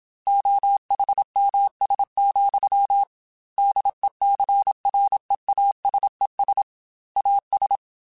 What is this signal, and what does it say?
Morse code, heard as a keyed tone, about 27 wpm, 800 Hz. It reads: OHMS, DECREASES AS
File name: Technical Subjects / Technical Subjects-mp3s / XC @25